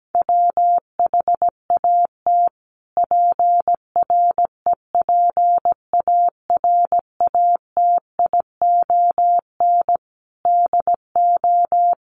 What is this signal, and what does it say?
Morse code, heard as a keyed tone, about 17 wpm, 700 Hz. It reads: WHAT PREPARATION DO
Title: Sentences from Top 1400 Words 17wpm